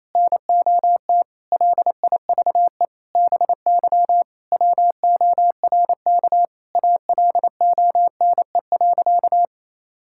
Morse code, heard as a keyed tone, about 28 wpm, 700 Hz. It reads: NOT LIVE BY WORK ALONE.
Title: Sentences from Top 1600 Words 28wpm